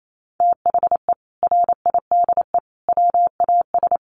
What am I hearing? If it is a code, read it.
Morse code, heard as a keyed tone, about 28 wpm, 700 Hz. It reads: THE RIDE WAS
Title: Sentences from Top 1200 Words 28wpm